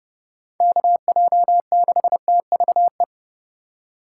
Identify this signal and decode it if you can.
Morse code, heard as a keyed tone, about 30 wpm, 700 Hz. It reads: KJ6TVE
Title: USA Call Signs - 6 Characters - 2x Character Spacing - 30wpm